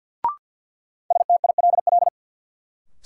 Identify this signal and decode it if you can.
Morse code, heard as a keyed tone, about 53 wpm, 700 Hz. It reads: STILL
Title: Sentences from Top 300 Words - New Word Drill - 50wpm